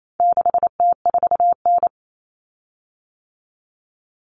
Morse code, heard as a keyed tone, about 28 wpm, 700 Hz. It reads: 6T4N
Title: Common Cut Numbers - 4-Digit 28wpm